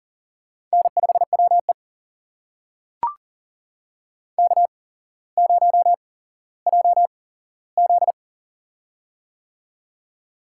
Morse code, heard as a keyed tone, about 40 wpm, spaced as for 17 wpm, 700 Hz. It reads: N5WE K0JZ
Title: USA Call Signs - 4 Characters - 8x Character Spacing - 40wpm